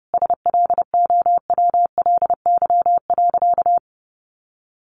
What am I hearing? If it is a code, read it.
Morse code, heard as a keyed tone, about 30 wpm, 700 Hz. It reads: SLOWLY.
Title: Sentences from Top 1100 Words 30wpm